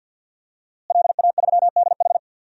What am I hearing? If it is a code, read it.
Morse code, heard as a keyed tone, about 51 wpm, 700 Hz. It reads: LA3DH
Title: DXCC-100 Call Signs - 5 Characters - Repeated 2x - 50wpm